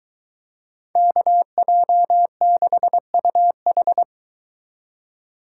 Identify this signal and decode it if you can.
Morse code, heard as a keyed tone, about 23 wpm, 700 Hz. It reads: KJ6UH